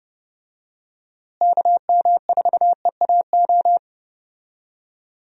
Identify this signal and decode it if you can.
Morse code, heard as a keyed tone, about 30 wpm, 700 Hz. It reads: KM4EAO
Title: USA Call Signs - 6 Characters - Repeated 3x - 30wpm